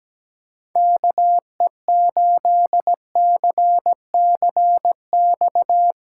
Morse code, heard as a keyed tone, about 17 wpm, 700 Hz. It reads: KE8CCX
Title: USA Callsign DE Callsign - 6 Characters - 17wpm